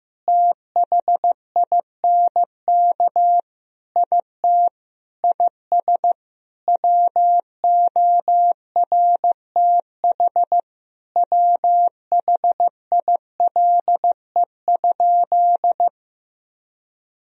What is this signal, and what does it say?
Morse code, heard as a keyed tone, about 15 wpm, 700 Hz. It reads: THINK IT IS WORTH WHILE?
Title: Sentences from Top 1000 Words - Review Part 1 - 15wpm